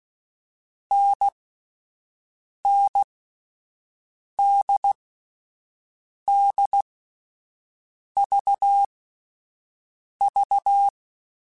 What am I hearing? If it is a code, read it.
Morse code, heard as a keyed tone, about 16 wpm, spaced as for 5 wpm, 800 Hz. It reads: NNDDVV
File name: Difficult Combos / Difficult Combos-mp3s / NDB NDB6 @5